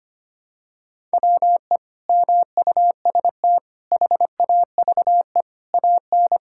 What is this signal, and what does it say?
Morse code, heard as a keyed tone, about 25 wpm, 700 Hz. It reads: WE MUST HAVE AN
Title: Sentences from Top 1000 Words - Review Part 6 - 25wpm